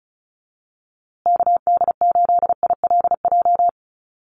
Morse code, heard as a keyed tone, about 34 wpm, 700 Hz. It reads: KD8ILJ
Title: USA Call Signs Repeated 3x 35wpm